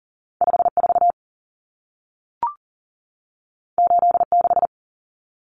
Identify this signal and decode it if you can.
Morse code, heard as a keyed tone, about 40 wpm, 700 Hz. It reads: 54 86
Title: Numbers - 2-Digit 40wpm